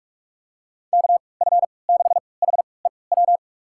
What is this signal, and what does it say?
Morse code, heard as a keyed tone, about 46 wpm, spaced as for 26 wpm, 700 Hz. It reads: KF6HEW
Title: USA Call Signs - 6 Characters - 3x Character Spacing - 45wpm